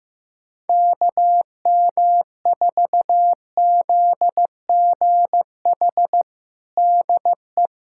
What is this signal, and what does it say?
Morse code, heard as a keyed tone, about 15 wpm, 700 Hz. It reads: KM4ZGH DE
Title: USA Callsign DE Callsign 15wpm